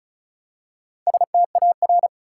Morse code, heard as a keyed tone, about 36 wpm, 700 Hz. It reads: STAR
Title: Four Letter Words - 35wpm